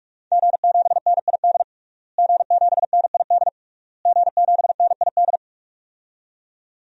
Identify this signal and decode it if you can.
Morse code, heard as a keyed tone, about 46 wpm, 700 Hz. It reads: G7NID G7NID G7NID
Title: DXCC-100 Call Signs - 5 Characters - Repeated 3x - 45wpm